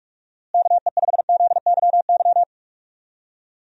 Morse code, heard as a keyed tone, about 44 wpm, 700 Hz. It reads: KE5ZYY
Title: USA Call Signs - 6 Characters - Repeated 2x - 45wpm